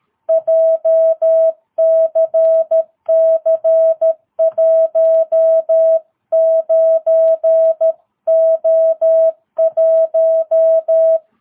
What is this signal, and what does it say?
Morse code, heard as a keyed tone, about 13 wpm, 660 Hz. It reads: JCC19O1
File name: jh8cbh / 2022cw